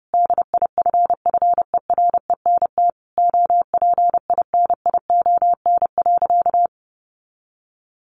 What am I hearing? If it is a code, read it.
Morse code, heard as a keyed tone, about 30 wpm, 700 Hz. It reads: DIFFERENT OPINION.
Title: Sentences from Top 1200 Words 30wpm